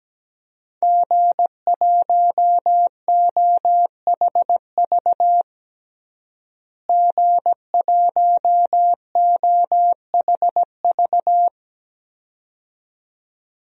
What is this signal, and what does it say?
Morse code, heard as a keyed tone, about 17 wpm, 700 Hz. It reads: G1OHV G1OHV
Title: DXCC-100 Call Signs - 5 Characters - Repeated 2x - 17wpm